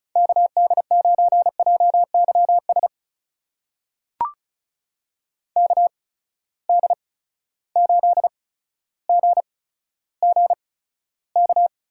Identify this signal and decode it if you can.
Morse code, heard as a keyed tone, about 35 wpm, spaced as for 15 wpm, 700 Hz. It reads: KD9JYS KD8GGK
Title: USA Call Signs - 6 Characters - 8x Character Spacing - 35wpm